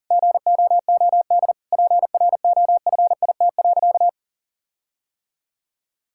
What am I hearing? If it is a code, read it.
Morse code, heard as a keyed tone, about 37 wpm, 700 Hz. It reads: GOOD PROFIT.
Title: Sentences from Top 1600 Words 40wpm